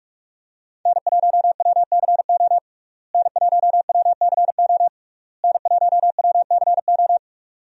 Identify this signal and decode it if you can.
Morse code, heard as a keyed tone, about 45 wpm, 700 Hz. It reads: N1WCO N1WCO N1WCO
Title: USA Call Signs - 5 Characters - Repeated 3x - 45wpm